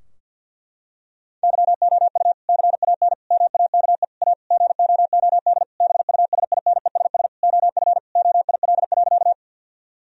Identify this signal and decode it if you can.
Morse code, heard as a keyed tone, about 49 wpm, 700 Hz. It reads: YOU CAN MAKE A GOOD BUSINESS OF OIL.
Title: Sentences from Top 1000 Words - Review Part 4 - 50wpm